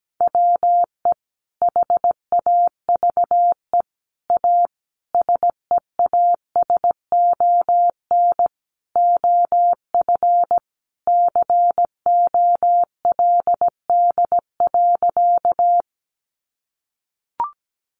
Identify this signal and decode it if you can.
Morse code, heard as a keyed tone, about 17 wpm, 700 Hz. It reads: WE HAVE A SEASON OF COLD.
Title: Sentences from Top 1400 Words 17wpm